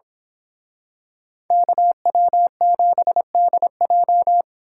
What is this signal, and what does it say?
Morse code, heard as a keyed tone, about 26 wpm, 700 Hz. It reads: KW7DJ
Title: Speed Racing - USA 2x2 Call Signs 22wpm